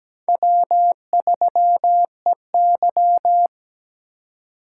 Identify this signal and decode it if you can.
Morse code, heard as a keyed tone, about 21 wpm, 700 Hz. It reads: W3EY
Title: Speed Racing - USA 1x2 Call Signs 17wpm